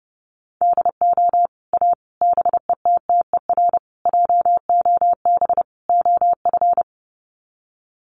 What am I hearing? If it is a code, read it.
Morse code, heard as a keyed tone, about 30 wpm, 700 Hz. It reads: DO A BETTER JOB OF
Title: Sets of 5 Words Encore2 30wpm